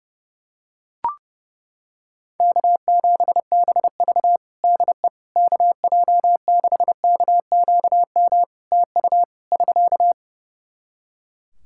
Morse code, heard as a keyed tone, about 32 wpm, 700 Hz. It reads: K7BV DE KJ6KQM TU <SK>
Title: Practice QSO - Exchange 4 - Ending - 30wpm